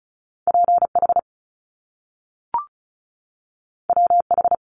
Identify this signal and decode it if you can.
Morse code, heard as a keyed tone, about 36 wpm, 700 Hz. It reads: PH WH
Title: Top Two Letter Combinations in Words 35wpm